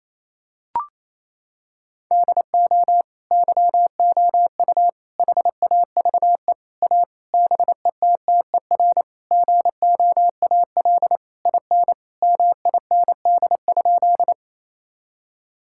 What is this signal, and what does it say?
Morse code, heard as a keyed tone, about 28 wpm, 700 Hz. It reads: DO YOU HAVE A BETTER GOAL IN MIND?
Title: Sentences from Top 1000 Words - Review Part 5 - 28wpm